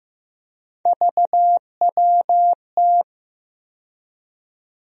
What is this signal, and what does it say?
Morse code, heard as a keyed tone, about 15 wpm, 700 Hz. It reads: VWT
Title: USA Call Sign Postfixes - 3 Letters 15wpm